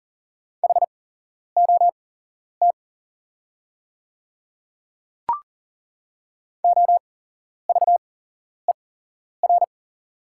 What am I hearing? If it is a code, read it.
Morse code, heard as a keyed tone, about 40 wpm, spaced as for 11 wpm, 690 Hz. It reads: HOT OVER
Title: 8x Character Spacing - Top 200 Words - 40wpm